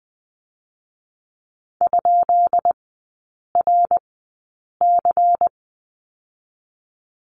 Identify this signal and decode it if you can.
Morse code, heard as a keyed tone, about 20 wpm, 700 Hz. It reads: ? R C
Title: 133 Lesson 22e - Question Mark - Three Characters 20wpm